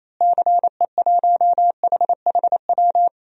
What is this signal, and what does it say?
Morse code, heard as a keyed tone, about 28 wpm, 700 Hz. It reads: CE1HHW
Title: DXCC-100 Call Signs - 6 Characters - Repeated 3x - 28wpm